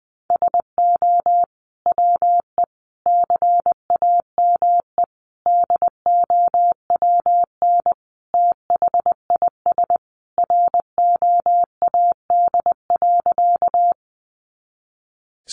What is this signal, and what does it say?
Morse code, heard as a keyed tone, about 19 wpm, 700 Hz. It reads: SO WE CAME DOWN THIS ROAD.
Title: Sentences from Top 400 Words 20wpm